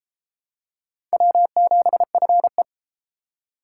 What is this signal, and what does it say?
Morse code, heard as a keyed tone, about 31 wpm, 700 Hz. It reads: W7FE